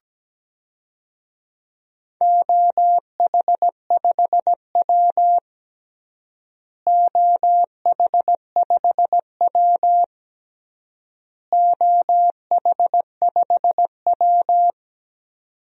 Morse code, heard as a keyed tone, about 17 wpm, 700 Hz. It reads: OH5W OH5W OH5W
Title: DXCC-100 Call Signs - 4 Characters - Repeated 3x - 17wpm